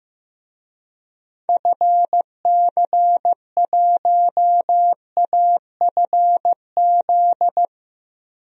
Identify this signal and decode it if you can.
Morse code, heard as a keyed tone, about 15 wpm, 700 Hz. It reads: FC1AFZ
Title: DXCC-100 Call Signs - 6 Characters - Repeated 2x - 15wpm